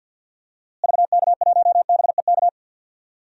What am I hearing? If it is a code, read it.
Morse code, heard as a keyed tone, about 49 wpm, 700 Hz. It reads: VK1BEK
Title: DXCC-100 Call Signs - 6 Characters - Repeated 3x - 50wpm